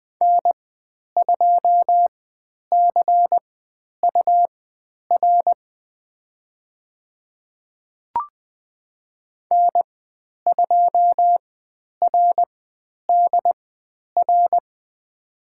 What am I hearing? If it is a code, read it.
Morse code, heard as a keyed tone, about 20 wpm, spaced as for 10 wpm, 700 Hz. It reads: N2CUR N2RDR
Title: 102 Lesson 17g - Letter u - Call Signs 20wpm